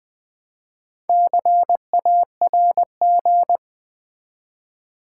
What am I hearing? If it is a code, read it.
Morse code, heard as a keyed tone, about 20 wpm, 700 Hz. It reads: CARG
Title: Build A Word - 1 Letter At A Time - 5-Letter Words - 20wpm